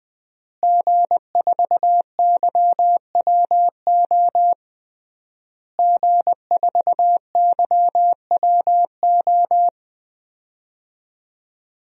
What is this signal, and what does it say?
Morse code, heard as a keyed tone, about 19 wpm, 700 Hz. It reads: G4YWO G4YWO
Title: DXCC-100 Call Signs - 5 Characters - Repeated 3x - 20wpm